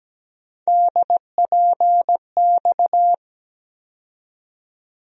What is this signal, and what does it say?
Morse code, heard as a keyed tone, about 17 wpm, 700 Hz. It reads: DPX
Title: USA Call Sign Postfixes - 3 Letters 17wpm